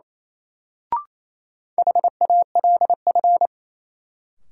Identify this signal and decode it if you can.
Morse code, heard as a keyed tone, about 28 wpm, 700 Hz. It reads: HALF
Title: Sets of 3 Words - New Word Drill - 28wpm